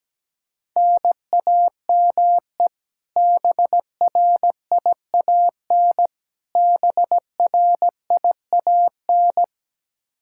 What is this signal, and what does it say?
Morse code, heard as a keyed tone, about 17 wpm, 700 Hz. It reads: NAME BRIAN BRIAN
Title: K1USN SST QSO Practice 17wpm